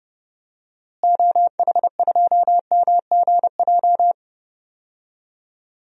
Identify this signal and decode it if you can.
Morse code, heard as a keyed tone, about 29 wpm, 700 Hz. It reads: OH2MGJ
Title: DXCC-100 Call Signs - 6 Characters - Repeated 3x - 30wpm